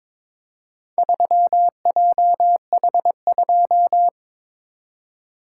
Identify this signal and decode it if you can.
Morse code, heard as a keyed tone, about 22 wpm, 700 Hz. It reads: 3JH2
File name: practice / 22wpm / Four Letter-Number Combinations - 22wpm